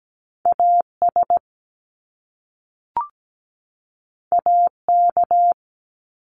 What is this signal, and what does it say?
Morse code, heard as a keyed tone, about 17 wpm, 700 Hz. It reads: AS AK
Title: Top Two Letter Combinations in Words 17wpm